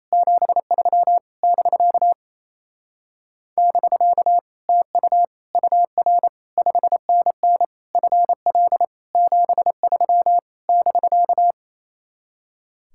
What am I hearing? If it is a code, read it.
Morse code, heard as a keyed tone, about 33 wpm, 700 Hz. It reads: 73 <BK> <BK> TU UR 5NN FL 73 <BK>
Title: Speed-Racing Common QSO Phrases Encore - 28wpm